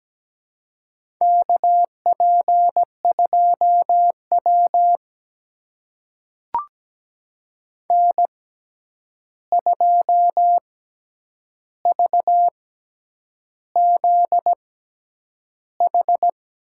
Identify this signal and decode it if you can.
Morse code, heard as a keyed tone, about 17 wpm, spaced as for 8 wpm, 700 Hz. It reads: KP2W N2VZH
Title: USA Call Signs - 6x Character Spacing - 17wpm